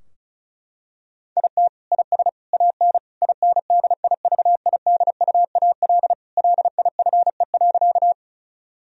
Morse code, heard as a keyed tone, about 36 wpm, 700 Hz. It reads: IT IS AN INDIVIDUAL LIFE.
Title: Sentences from Top 1000 Words - Review Part 4 - 35wpm